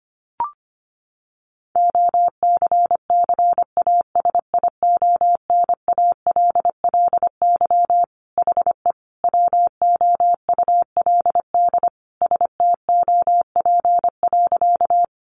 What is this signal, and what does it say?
Morse code, heard as a keyed tone, about 25 wpm, 700 Hz. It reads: OCCASIONALLY HE WOULD STOP.
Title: Sentences from Top 1300 Words 25wpm